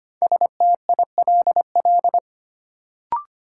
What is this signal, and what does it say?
Morse code, heard as a keyed tone, about 25 wpm, 700 Hz. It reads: STILL
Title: Sentences from Top 300 Words - New Word Drill - 25wpm